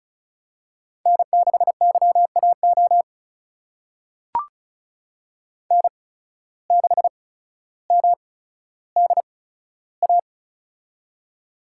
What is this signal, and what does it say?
Morse code, heard as a keyed tone, about 34 wpm, spaced as for 15 wpm, 700 Hz. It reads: N6YAO N6MDA
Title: USA Call Signs - 5 Characters - 8x Character Spacing - 35wpm